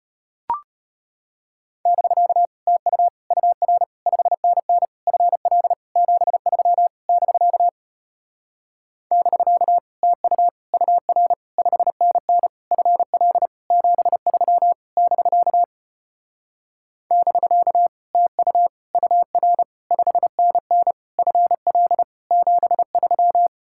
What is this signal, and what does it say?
Morse code, heard as a keyed tone, about 33 wpm, 700 Hz. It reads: <BK> TU UR 5NN FL 73 <BK> <BK> TU UR 5NN FL 73 <BK> <BK> TU UR 5NN FL 73